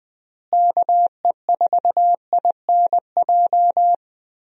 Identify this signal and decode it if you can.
Morse code, heard as a keyed tone, about 20 wpm, 700 Hz. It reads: KE4INJ
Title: USA Call Signs - 6 Characters - Repeated 3x - 20wpm